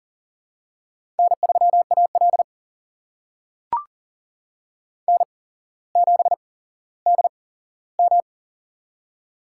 Morse code, heard as a keyed tone, about 40 wpm, spaced as for 17 wpm, 700 Hz. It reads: N3AL N7DM
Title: USA Call Signs - 4 Characters - 8x Character Spacing - 40wpm